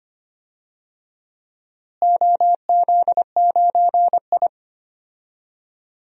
Morse code, heard as a keyed tone, about 25 wpm, 700 Hz. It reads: OZ9I
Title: DXCC-100 Call Signs - 4 Characters - Repeated 3x - 25wpm